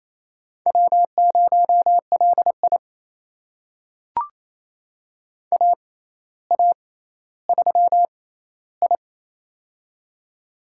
Morse code, heard as a keyed tone, about 28 wpm, spaced as for 14 wpm, 700 Hz. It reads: W0LI AA3I
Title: USA Call Signs - 4 Characters - 6x Character Spacing - 28wpm